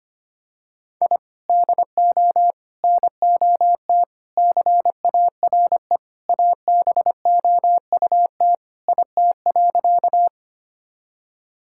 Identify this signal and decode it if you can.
Morse code, heard as a keyed tone, about 25 wpm, 700 Hz. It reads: I DO NOT CARE ABOUT IT.
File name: practice / 25wpm / Sentences with 4-Word Phrases - 25wpm